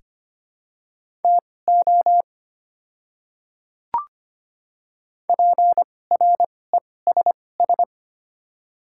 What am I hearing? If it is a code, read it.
Morse code, heard as a keyed tone, about 25 wpm, spaced as for 18 wpm, 700 Hz. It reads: TO PRESS
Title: 2x Character Spacing - Top 300 Words - 25wpm